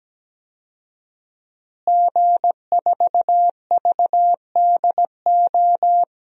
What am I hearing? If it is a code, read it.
Morse code, heard as a keyed tone, about 17 wpm, 700 Hz. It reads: G4VDO
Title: DXCC-100 Call Signs - 5 Characters - Repeated 3x - 17wpm